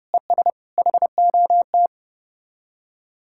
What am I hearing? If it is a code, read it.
Morse code, heard as a keyed tone, about 33 wpm, 700 Hz. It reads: ES HOT